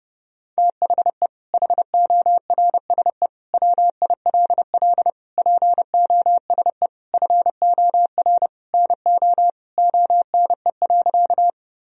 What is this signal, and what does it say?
Morse code, heard as a keyed tone, about 30 wpm, 700 Hz. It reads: THE HORSE WILL POSE FOR NO ONE.
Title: Sentences from Top 1000 Words - Review Part 2 - 30wpm